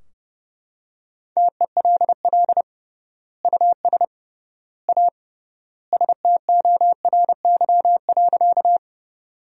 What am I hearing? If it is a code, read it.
Morse code, heard as a keyed tone, about 32 wpm, 700 Hz. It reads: TELL US A STORY.
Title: 3x Word-Spacing - Sentences from Top 300 Words - 30wpm